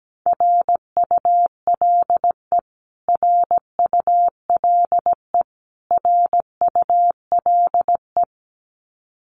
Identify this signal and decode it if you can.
Morse code, heard as a keyed tone, about 17 wpm, 700 Hz. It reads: RULE RULE RULE
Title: Top 500 Words Repeated 17wpm